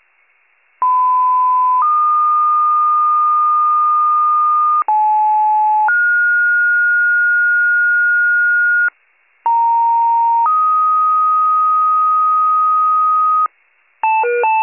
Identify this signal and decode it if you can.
Two-tone page, heard as a one-second tone, then a three-second tone, about 1010 then 1220 Hz, about 860 then 1430 Hz, about 930 then 1250 Hz. It is heard from 0.8 to 13.5 s.